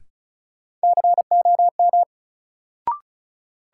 Morse code, heard as a keyed tone, about 36 wpm, 700 Hz. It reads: COM
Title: Sentences from Top 1300 Words - New Word Drill - 35wpm